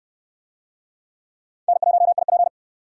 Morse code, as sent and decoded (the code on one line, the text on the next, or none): -. .---- .. .--.
N1IP